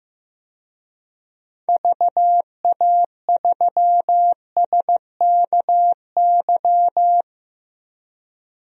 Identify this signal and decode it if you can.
Morse code, heard as a keyed tone, about 15 wpm, 700 Hz. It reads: VA3SKY
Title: DXCC-100 Call Signs - 6 Characters - Repeated 3x - 15wpm